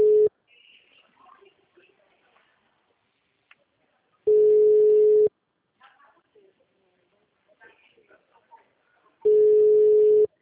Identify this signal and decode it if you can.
Ringback tone (Europe, 425 Hz), with speech.
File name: out- -2011- - - .70589